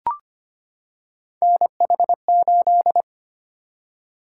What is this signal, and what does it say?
Morse code, heard as a keyed tone, about 25 wpm, 700 Hz. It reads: NH8
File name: Three Letter-Number Combinations - 25wpm